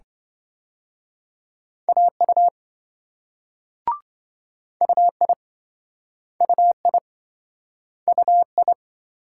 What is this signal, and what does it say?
Morse code, heard as a keyed tone, about 26 wpm, 700 Hz. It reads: AU UI UI UI